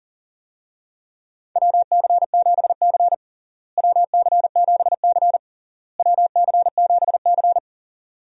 Morse code, heard as a keyed tone, about 40 wpm, 700 Hz. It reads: WC7C WC7C WC7C
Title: USA Call Signs - 4 Characters - Repeated 3x - 40wpm